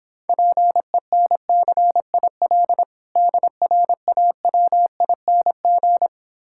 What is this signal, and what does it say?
Morse code, heard as a keyed tone, about 26 wpm, 700 Hz. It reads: PENCIL DRAWING